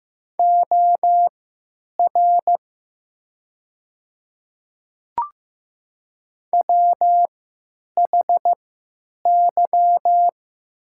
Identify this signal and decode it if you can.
Morse code, heard as a keyed tone, about 15 wpm, spaced as for 9 wpm, 700 Hz. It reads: OR WHY
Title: 3x Character Spacing - Top 300 Words - 15wpm